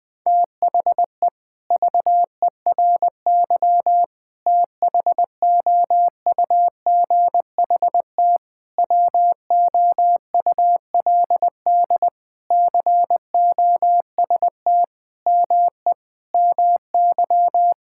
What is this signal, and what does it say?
Morse code, heard as a keyed tone, about 20 wpm, 700 Hz. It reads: THE VERY THOUGHT WOULD COST ME MY
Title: Sentences from Top 1000 Words - Review Part 5 - 20wpm